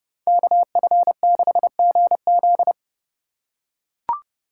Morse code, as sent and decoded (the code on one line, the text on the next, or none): -.- ..-. -.... --. --..
KF6GZ